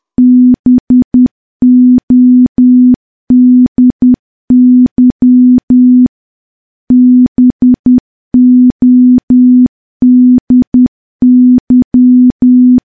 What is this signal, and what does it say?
Morse code, heard as a keyed tone, about 10 wpm, 250 Hz. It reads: BODY BODY
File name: morse / morse-training-10WPM